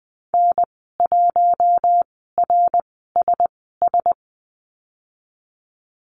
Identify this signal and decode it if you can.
Morse code, heard as a keyed tone, about 20 wpm, spaced as for 15 wpm, 700 Hz. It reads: N1RSS
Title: 076 Lesson 13g - Letter L - Call Signs 20wpm